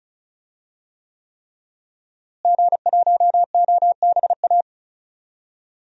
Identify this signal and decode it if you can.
Morse code, heard as a keyed tone, about 36 wpm, 700 Hz. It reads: G1OBA
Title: DXCC-100 Call Signs - 5 Characters - Repeated 3x - 35wpm